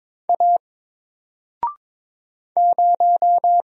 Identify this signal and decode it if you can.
Morse code, heard as a keyed tone, about 22 wpm, 700 Hz. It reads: A 0
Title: Single Letter-Number Alt 22wpm